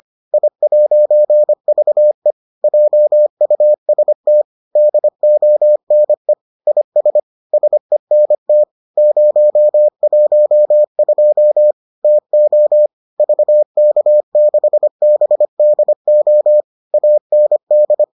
Morse code, heard as a keyed tone, about 25 wpm, 600 Hz. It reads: I'VE JUST DONE IS SENT 012 TO VK6BDO AND